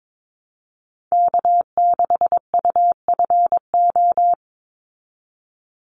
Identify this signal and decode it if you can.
Morse code, heard as a keyed tone, about 22 wpm, 700 Hz. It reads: K6UFO
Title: Call Signs Repeated 3x - 22wpm